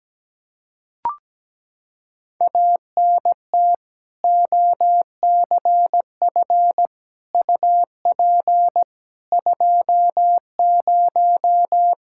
Morse code, heard as a keyed tone, about 17 wpm, 700 Hz. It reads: ANT OCF UP 20
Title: K1USN SST QSO Practice 17wpm